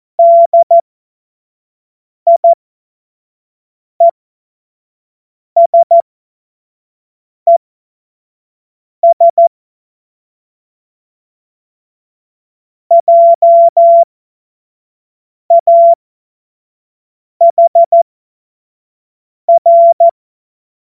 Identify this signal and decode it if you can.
Morse code, heard as a keyed tone, about 14 wpm, spaced as for 5 wpm, 680 Hz. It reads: DIESES JAHR
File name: Morsekurs / Witze / Witz 14. März 2023 25 F70